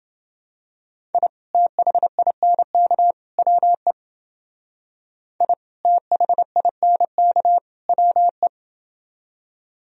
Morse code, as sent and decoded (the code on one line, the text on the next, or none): .. / - .... .. -. -.- / .-- . / .. / - .... .. -. -.- / .-- .
I THINK WE I THINK WE